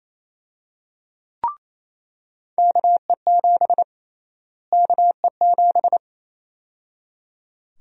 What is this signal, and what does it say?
Morse code, heard as a keyed tone, about 28 wpm, 700 Hz. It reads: KE7 KE7
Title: USA Call Sign Prefixes - Repeated 2x - 28wpm